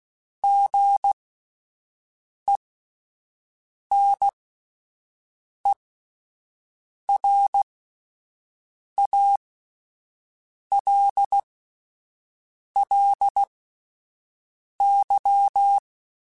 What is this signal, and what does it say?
Morse code, heard as a keyed tone, about 16 wpm, spaced as for 5 wpm, 800 Hz. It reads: GENERALLY
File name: Spaced Characters / Spaced Characters-mp3s / Weather 5 1 sp @10